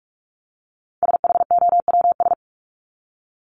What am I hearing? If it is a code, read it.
Morse code, heard as a keyed tone, about 44 wpm, 700 Hz. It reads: SHOWS
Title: ES:S Suffix - Top 100 Words with Base Words 45wpm